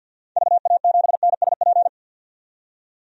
Morse code, heard as a keyed tone, about 51 wpm, 700 Hz. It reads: VA7NSP
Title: DXCC-100 Call Signs - 6 Characters - 50wpm